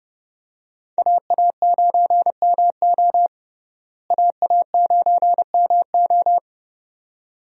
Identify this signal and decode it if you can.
Morse code, heard as a keyed tone, about 30 wpm, 700 Hz. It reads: AA9MO AA9MO
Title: USA Call Signs - 5 Characters - Repeated 2x - 30wpm